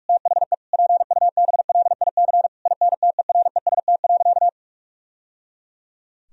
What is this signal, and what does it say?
Morse code, heard as a keyed tone, about 41 wpm, 700 Hz. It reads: THE PUBLIC INTEREST.
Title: Sentences from Top 1000 Words - Review Part 4 - 45wpm